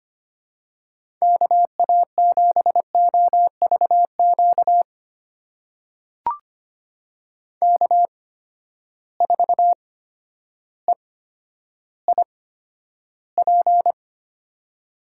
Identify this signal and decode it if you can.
Morse code, heard as a keyed tone, about 25 wpm, spaced as for 11 wpm, 700 Hz. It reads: KA7OVQ K4EIP
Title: USA Call Signs - 8x Character Spacing - 25wpm